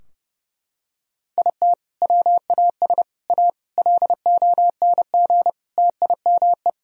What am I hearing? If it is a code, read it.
Morse code, heard as a keyed tone, about 30 wpm, 700 Hz. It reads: IT WAS A LONG TIME
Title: Sentences from Top 1000 Words - Review Part 2 - 30wpm